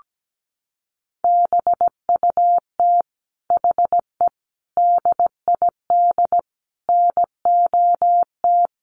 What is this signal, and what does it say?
Morse code, heard as a keyed tone, about 17 wpm, 700 Hz. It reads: BUT HE DID NOT
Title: Sentences from Top 1300 Words 17wpm